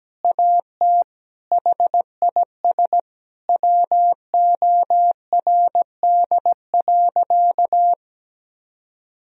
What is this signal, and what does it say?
Morse code, heard as a keyed tone, about 17 wpm, 700 Hz. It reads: AT HIS WORD.
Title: Sentences from Top 1000 Words - Review Part 1 - 17wpm